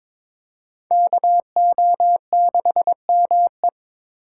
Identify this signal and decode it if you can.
Morse code, heard as a keyed tone, about 26 wpm, 700 Hz. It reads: KO6ME
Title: Speed Racing - USA 2x2 Call Signs 22wpm